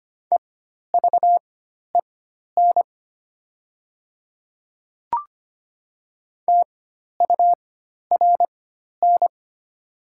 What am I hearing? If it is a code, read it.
Morse code, heard as a keyed tone, about 25 wpm, spaced as for 12 wpm, 700 Hz. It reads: EVEN TURN
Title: 4x Character Spacing - Top 200 Words - 25wpm